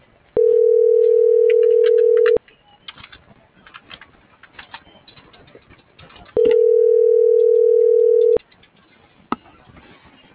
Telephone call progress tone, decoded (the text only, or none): ringback tone